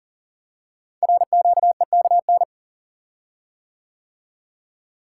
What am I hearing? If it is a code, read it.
Morse code, heard as a keyed tone, about 39 wpm, 690 Hz. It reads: RQEKN
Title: Code Groups - 5 Letters - 40wpm